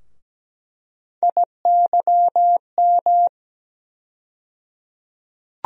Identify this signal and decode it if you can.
Morse code, heard as a keyed tone, about 17 wpm, 700 Hz. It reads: IYM
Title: USA Call Sign Postfixes - 3 Letters 17wpm